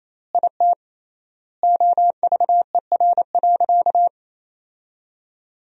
Morse code, heard as a keyed tone, about 27 wpm, 700 Hz. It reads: IT OVER.
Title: 3x Word-Spacing - Sentences from Top 500 Words - 28wpm